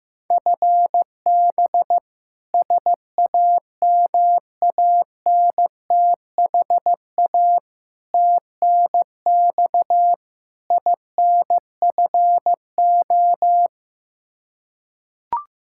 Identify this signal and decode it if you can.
Morse code, heard as a keyed tone, about 15 wpm, 700 Hz. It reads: FB SAMANTHA TNX INFO
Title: K1USN SST QSO Practice 15wpm